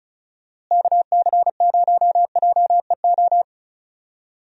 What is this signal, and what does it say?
Morse code, heard as a keyed tone, about 36 wpm, 700 Hz. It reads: KC0JEO
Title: USA Call Signs - 6 Characters - Repeated 3x - 35wpm